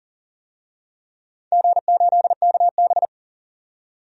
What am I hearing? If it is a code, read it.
Morse code, heard as a keyed tone, about 40 wpm, 700 Hz. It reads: G8KB
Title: DXCC-100 Call Signs - 4 Characters - 40wpm